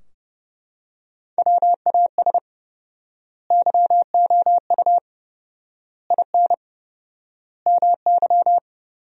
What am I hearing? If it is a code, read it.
Morse code, heard as a keyed tone, about 30 wpm, 700 Hz. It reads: WAS YOU IN MY
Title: 4x Word-Spacing - Sentences from Top 300 Words - 30wpm